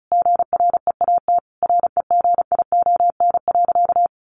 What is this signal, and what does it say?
Morse code, heard as a keyed tone, about 34 wpm, 700 Hz. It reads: GREAT REGION.
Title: Sentences from Top 1500 Words 35wpm